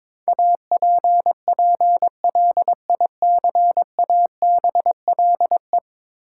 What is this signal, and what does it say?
Morse code, heard as a keyed tone, about 21 wpm, 700 Hz. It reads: APPLICABLE
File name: practice / 22wpm / Ten Letter Words - 22wpm